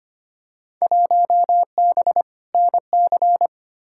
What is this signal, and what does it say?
Morse code, heard as a keyed tone, about 25 wpm, 700 Hz. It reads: 1B NC
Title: Field Day Exchanges - 25wpm